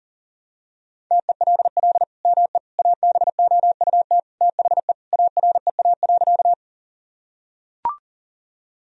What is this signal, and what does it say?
Morse code, heard as a keyed tone, about 40 wpm, 700 Hz. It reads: TELL ME ABOUT THE AREA.
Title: Sentences from Top 1000 Words - Review Part 2 - 40wpm